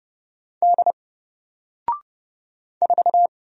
Morse code, heard as a keyed tone, about 30 wpm, 700 Hz. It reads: D 4
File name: Single-Letter-Number-30wpm